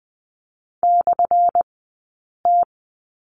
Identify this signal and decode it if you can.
Morse code, heard as a keyed tone, about 20 wpm, spaced as for 8 wpm, 690 Hz. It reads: /T
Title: 217 Lesson 35d - Number 0 - Two Characters 20wpm